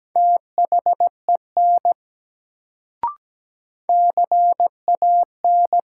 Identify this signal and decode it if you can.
Morse code, heard as a keyed tone, about 17 wpm, 700 Hz. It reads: THEN CAN
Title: Sentences from Top 100 Words - New Word Drill - 17wpm